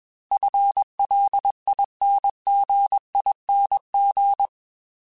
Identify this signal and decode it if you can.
Morse code, heard as a keyed tone, about 21 wpm, 800 Hz. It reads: FLINGING
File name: FL / FL mp3s / FL Prefixes 3x2 @20